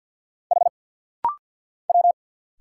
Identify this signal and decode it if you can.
Morse code, heard as a keyed tone, about 50 wpm, 690 Hz. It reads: H W
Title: Single Letter-Number Alt - Rapid-Fire 50wpm